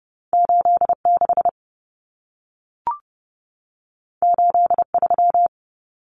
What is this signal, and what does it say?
Morse code, heard as a keyed tone, about 30 wpm, 700 Hz. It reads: 86 83
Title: Numbers - 2-Digit 30wpm